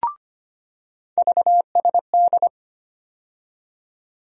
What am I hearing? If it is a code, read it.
Morse code, heard as a keyed tone, about 25 wpm, 700 Hz. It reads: VSD
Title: USA Call Sign Postfixes - 3 Letters 25wpm